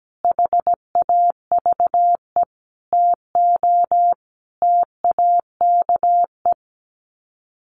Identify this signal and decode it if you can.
Morse code, heard as a keyed tone, about 17 wpm, 700 Hz. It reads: HAVE TO TAKE
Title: Sets of 5 Words Encore 17wpm